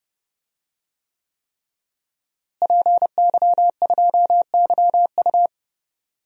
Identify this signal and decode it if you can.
Morse code, heard as a keyed tone, about 30 wpm, 700 Hz. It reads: PY2YU
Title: DXCC-100 Call Signs - 5 Characters - Repeated 3x - 30wpm